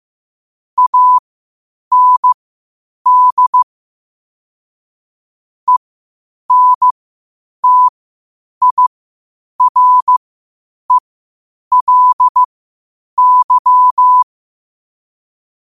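Morse code, heard as a keyed tone, about 15 wpm, spaced as for 8 wpm, 1000 Hz. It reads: AND ENTIRELY
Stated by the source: Morse code at 8 wpm.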